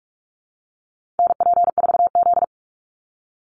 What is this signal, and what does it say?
Morse code, heard as a keyed tone, about 44 wpm, 700 Hz. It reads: NP4Z
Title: Call Signs Repeated 2x - 45wpm